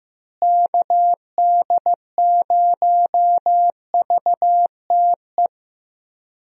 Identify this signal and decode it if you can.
Morse code, heard as a keyed tone, about 15 wpm, 700 Hz. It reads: KD0VTE
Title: USA Call Signs - 6 Characters - Repeated 3x - 15wpm